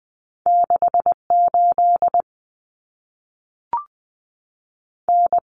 Morse code, heard as a keyed tone, about 20 wpm, 700 Hz. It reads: 68 N
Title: Common Cut Numbers - 2-Digit 20wpm